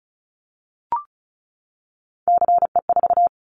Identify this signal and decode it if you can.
Morse code, heard as a keyed tone, about 34 wpm, 700 Hz. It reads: CE4
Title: Callsign Prefixes for Top 100 DXCC - 35wpm